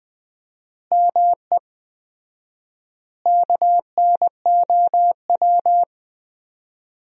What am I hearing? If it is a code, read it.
Morse code, heard as a keyed tone, about 21 wpm, 700 Hz. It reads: ME KNOW
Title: 4x Word-Spacing - Sentences from Top 300 Words - 20wpm